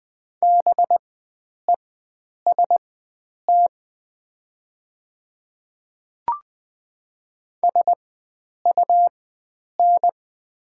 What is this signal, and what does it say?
Morse code, heard as a keyed tone, about 20 wpm, spaced as for 9 wpm, 700 Hz. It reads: BEST SUN
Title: 4x Character Spacing - Top 500 Words - 20wpm